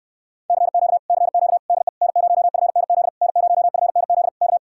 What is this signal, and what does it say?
Morse code, heard as a keyed tone, about 69 wpm, 700 Hz. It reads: CQ CQ DE N0FNZ N0FNZ K